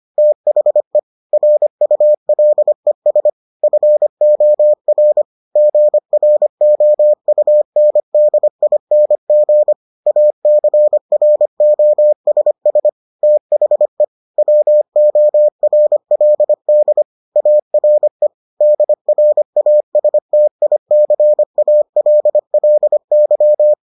Morse code, heard as a keyed tone, about 25 wpm, 600 Hz. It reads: THE RULES FOR GROUNDING ACROSS THE WORLD ARE DRASTICALLY